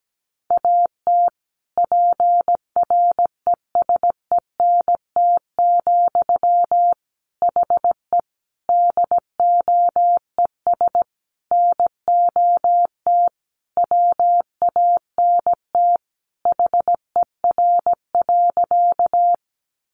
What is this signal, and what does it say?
Morse code, heard as a keyed tone, about 17 wpm, 700 Hz. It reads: AT PRESENT, HE DOES NOT WANT HER.
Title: Sentences from Top 800 Words 17wpm